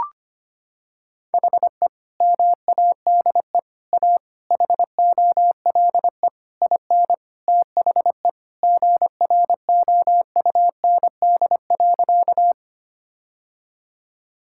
Morse code, heard as a keyed tone, about 25 wpm, 700 Hz. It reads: HE MADE A HOLE IN THE GROUND.
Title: Sentences from Top 1600 Words 25wpm